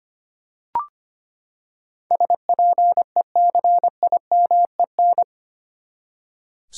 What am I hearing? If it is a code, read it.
Morse code, heard as a keyed tone, about 25 wpm, 700 Hz. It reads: SPECIMEN
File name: Eight Letter Words - 25wpm